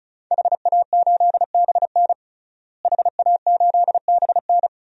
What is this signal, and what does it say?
Morse code, heard as a keyed tone, about 36 wpm, 700 Hz. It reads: HA8BN HA8BN
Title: DXCC-100 Call Signs - 5 Characters - Repeated 2x - 35wpm